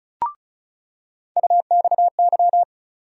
Morse code, heard as a keyed tone, about 34 wpm, 700 Hz. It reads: UXY
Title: USA Call Sign Postfixes - 3 Letters 35wpm